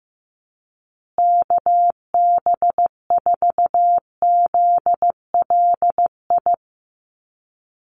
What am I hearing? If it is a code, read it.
Morse code, heard as a keyed tone, about 15 wpm, 700 Hz. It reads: KB4ZLI
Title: USA Call Signs Repeated 3x 15wpm